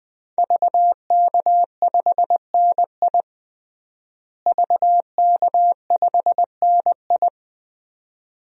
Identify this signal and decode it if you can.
Morse code, heard as a keyed tone, about 21 wpm, 700 Hz. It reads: VK5NI VK5NI
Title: DXCC-100 Call Signs - 5 Characters - Repeated 2x - 20wpm